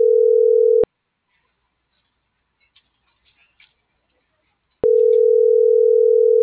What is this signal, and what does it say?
Ringback tone (North America), 440 + 480 Hz.
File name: CENTRAL MAERSK / out isabel.bravo - .75131